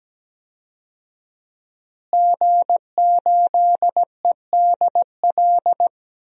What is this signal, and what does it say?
Morse code, heard as a keyed tone, about 17 wpm, 700 Hz. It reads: G8EDL
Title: DXCC-100 Call Signs - 5 Characters - Repeated 2x - 17wpm